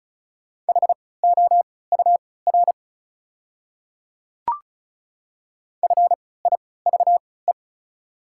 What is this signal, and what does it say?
Morse code, heard as a keyed tone, about 36 wpm, spaced as for 20 wpm, 700 Hz. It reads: HOUR FIVE
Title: 3x Character Spacing - Top 500 Words - 35wpm